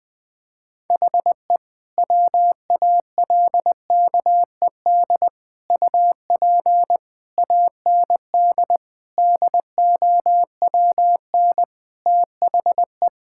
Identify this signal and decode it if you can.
Morse code, heard as a keyed tone, about 20 wpm, 700 Hz. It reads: HE WALKED UP AND DOWN THE
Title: Sentences with 4-Word Phrases - Spoken Beginning - 20wpm